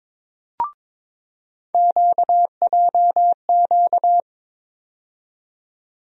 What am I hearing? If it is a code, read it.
Morse code, heard as a keyed tone, about 22 wpm, 700 Hz. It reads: QJQ
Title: USA Call Sign Postfixes - 3 Letters 22wpm